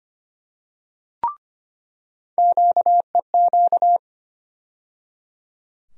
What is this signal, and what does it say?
Morse code, heard as a keyed tone, about 25 wpm, 700 Hz. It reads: QEQ
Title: USA Call Sign Postfixes - 3 Letters 25wpm